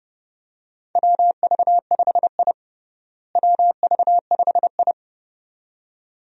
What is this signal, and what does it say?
Morse code, heard as a keyed tone, about 32 wpm, 700 Hz. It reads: WV5I WV5I
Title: USA Call Signs - 4 Characters - Repeated 2x - 30wpm